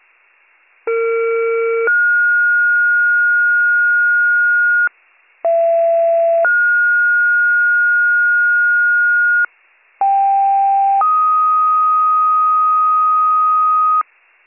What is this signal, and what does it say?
Two-tone page, heard as a one-second tone, then a three-second tone, about 460 then 1430 Hz, about 670 then 1430 Hz, about 790 then 1250 Hz, with noise. It is heard from 0.9 to 14.0 s.